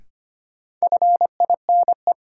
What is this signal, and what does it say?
Morse code, heard as a keyed tone, about 25 wpm, 700 Hz. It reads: FINE
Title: Sentences from Top 500 Words - New Word Drill - 25wpm